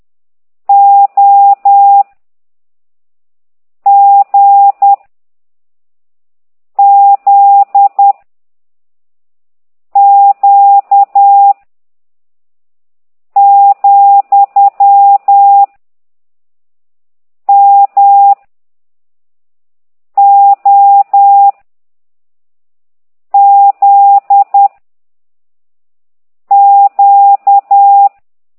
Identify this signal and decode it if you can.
Morse code, heard as a keyed tone, about 10 wpm, 800 Hz. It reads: O G Z Q , M O Z Q
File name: ct2idw / exerc15